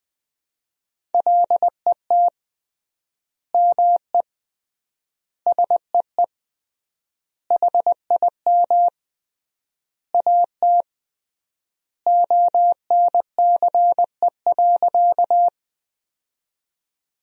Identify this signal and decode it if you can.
Morse code, heard as a keyed tone, about 21 wpm, 700 Hz. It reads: LET ME SEE HIM AT ONCE.
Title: 3x Word-Spacing - Sentences from Top 300 Words - 20wpm